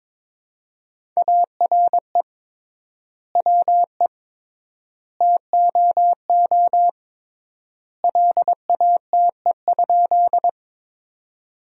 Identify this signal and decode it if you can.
Morse code, heard as a keyed tone, about 22 wpm, 700 Hz. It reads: ARE WE TOO LATE?
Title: 3x Word-Spacing - Sentences from Top 300 Words - 22wpm